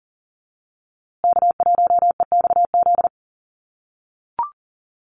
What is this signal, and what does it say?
Morse code, heard as a keyed tone, about 40 wpm, 700 Hz. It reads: K1EXZ
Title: USA Call Signs Repeated 2x 40wpm